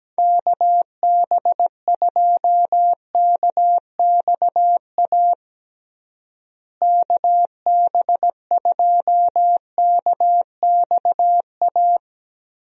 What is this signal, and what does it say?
Morse code, heard as a keyed tone, about 17 wpm, 700 Hz. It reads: KB2KXA KB2KXA
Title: USA Call Signs - 6 Characters - Repeated 3x - 17wpm